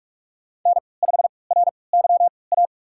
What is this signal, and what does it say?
Morse code, heard as a keyed tone, about 46 wpm, spaced as for 26 wpm, 700 Hz. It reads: N5RYA
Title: USA Call Signs - 5 Characters - 3x Character Spacing - 45wpm